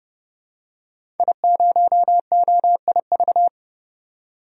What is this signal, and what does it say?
Morse code, heard as a keyed tone, about 30 wpm, 700 Hz. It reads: I0OIV
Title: DXCC-100 Call Signs - 5 Characters - 30wpm